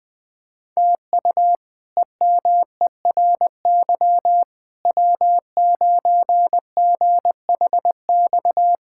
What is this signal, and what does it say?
Morse code, heard as a keyed tone, about 20 wpm, 700 Hz. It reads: TU EMERY W9GHX
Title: K1USN SST QSO Practice - 20wpm